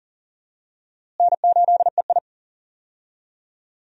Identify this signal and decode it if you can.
Morse code, heard as a keyed tone, about 38 wpm, 700 Hz. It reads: N8EI
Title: USA Call Signs - 4 Characters - Repeated 2x - 40wpm